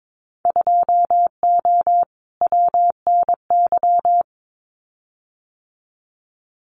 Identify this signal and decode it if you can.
Morse code, heard as a keyed tone, about 22 wpm, 700 Hz. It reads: 2O WNY
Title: Winter Field Day Exchanges Encore 22wpm